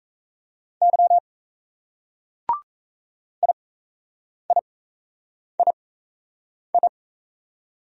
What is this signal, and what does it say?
Morse code, heard as a keyed tone, about 35 wpm, spaced as for 8 wpm, 700 Hz. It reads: Y IIII
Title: Speed-Racing Single Letters 28wpm